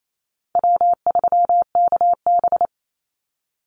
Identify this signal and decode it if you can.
Morse code, heard as a keyed tone, about 28 wpm, 700 Hz. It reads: W3KB
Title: Easy Call Signs Repeated 3x - 28wpm